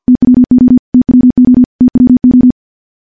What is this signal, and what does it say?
Morse code, heard as a keyed tone, about 51 wpm, 250 Hz. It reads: TWO TWO TWO
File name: morse / morse-training-50WPM